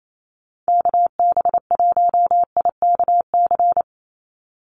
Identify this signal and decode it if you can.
Morse code, heard as a keyed tone, about 28 wpm, 700 Hz. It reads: KB1IKC
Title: USA Call Signs 28wpm